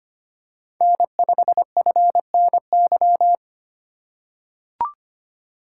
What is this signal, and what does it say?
Morse code, heard as a keyed tone, about 25 wpm, 700 Hz. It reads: N5FNY
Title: USA Call Signs - 5 Characters - 3x Character Spacing - 25wpm